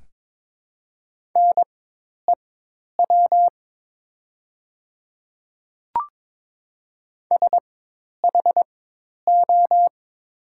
Morse code, heard as a keyed tone, about 23 wpm, spaced as for 10 wpm, 700 Hz. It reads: NEW SHO
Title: 4x Character Spacing - Top 300 Words - 22wpm